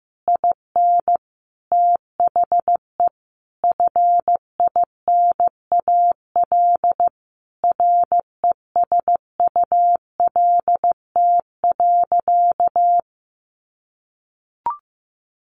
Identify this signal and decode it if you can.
Morse code, heard as a keyed tone, about 15 wpm, 700 Hz. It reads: IN THE FINAL RESULT.
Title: Sentences from Top 800 Words 15wpm